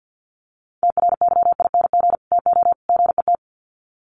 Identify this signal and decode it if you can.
Morse code, heard as a keyed tone, about 48 wpm, 700 Hz. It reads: TRYING TO GET
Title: Sets of 3 Words 50wpm